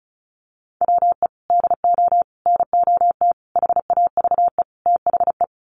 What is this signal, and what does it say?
Morse code, heard as a keyed tone, about 34 wpm, 700 Hz. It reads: WE DO NOT HAVE THE
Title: Sets of 5 Words 35wpm